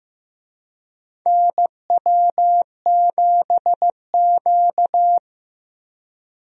Morse code, heard as a keyed tone, about 15 wpm, 700 Hz. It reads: NW7Q
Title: USA Call Signs - 4 Characters - Repeated 3x - 15wpm